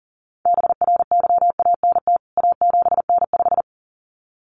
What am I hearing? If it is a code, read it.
Morse code, heard as a keyed tone, about 40 wpm, 700 Hz. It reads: BRYANT A7N5
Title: CWT Exchange 40wpm